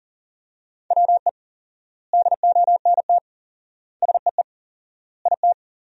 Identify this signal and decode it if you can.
Morse code, heard as a keyed tone, about 37 wpm, 700 Hz. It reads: WE DONT SEE IT
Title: 4x Word-Spacing - Sentences from Top 300 Words - 40wpm